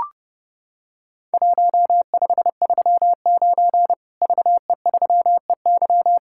Morse code, heard as a keyed tone, about 32 wpm, 700 Hz. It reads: 1539 VE3EY
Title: Worked All Europe DX Contest - QTC Exchange - 30wpm